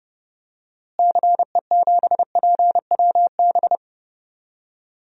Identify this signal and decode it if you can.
Morse code, heard as a keyed tone, about 30 wpm, 700 Hz. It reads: CE7PWB
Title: DXCC-100 Call Signs - 6 Characters - Repeated 2x - 30wpm